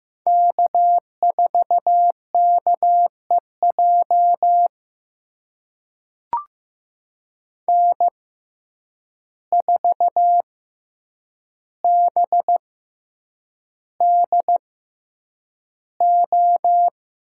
Morse code, heard as a keyed tone, about 15 wpm, spaced as for 8 wpm, 700 Hz. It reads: K4KEJ N4BDO
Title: USA Call Signs - 5 Characters - 6x Character Spacing - 15wpm